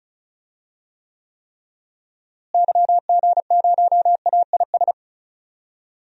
Morse code, heard as a keyed tone, about 34 wpm, 700 Hz. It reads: YG0AIS
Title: DXCC-100 Call Signs - 6 Characters - Repeated 3x - 35wpm